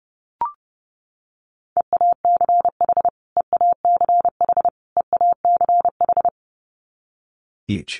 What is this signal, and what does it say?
Morse code, heard as a keyed tone, about 30 wpm, 700 Hz. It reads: EACH EACH EACH
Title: Top 500 Words Repeated 30wpm